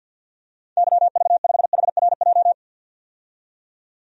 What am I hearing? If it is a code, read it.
Morse code, heard as a keyed tone, about 49 wpm, 700 Hz. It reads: YV5HRJ
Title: DXCC-100 Call Signs - 6 Characters - 50wpm